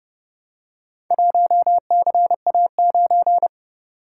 Morse code, heard as a keyed tone, about 29 wpm, 700 Hz. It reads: 1CA9